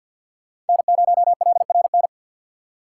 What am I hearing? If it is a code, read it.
Morse code, heard as a keyed tone, about 54 wpm, 700 Hz. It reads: N0LRN
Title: USA Call Signs - 6x Character Spacing - 50wpm